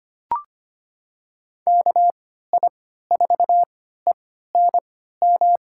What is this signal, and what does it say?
Morse code, heard as a keyed tone, about 25 wpm, spaced as for 14 wpm, 700 Hz. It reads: KI4ENM
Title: USA Call Signs - 6 Characters - 3x Character Spacing - 25wpm